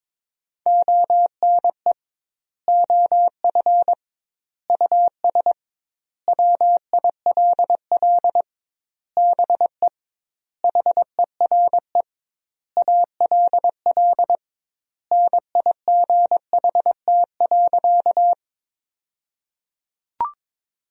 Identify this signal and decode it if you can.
Morse code, heard as a keyed tone, about 22 wpm, 700 Hz. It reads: ONE OF US WILL BE HERE ALL NIGHT.
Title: 2x Word-Spacing - Sentences from Top 300 Words - 22wpm